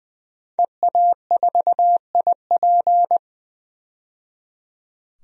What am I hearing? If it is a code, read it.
Morse code, heard as a keyed tone, about 20 wpm, 700 Hz. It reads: EA4IP
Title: DXCC-100 Call Signs - 5 Characters - 20wpm